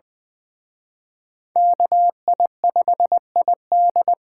Morse code, heard as a keyed tone, about 24 wpm, 700 Hz. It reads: KI5ID
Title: Speed Racing - USA 2x2 Call Signs 20wpm